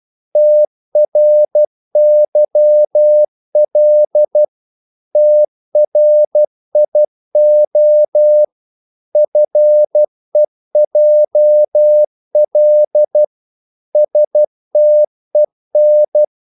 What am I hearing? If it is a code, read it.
Morse code, heard as a keyed tone, about 12 wpm, 600 Hz. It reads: TRYL TRIO FEJL STEN
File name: Korte ord DK 12wpm